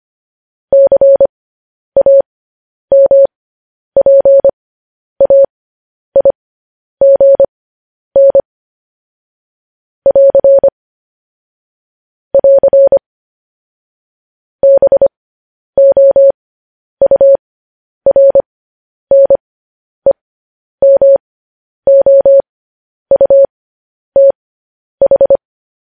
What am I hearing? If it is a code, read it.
Morse code, heard as a keyed tone, about 25 wpm, spaced as for 10 wpm, 550 Hz. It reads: CAMPAIGN <AR> <AR> BOURNEMOUTH
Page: news-headlines-in-morse-code-at-10-wpm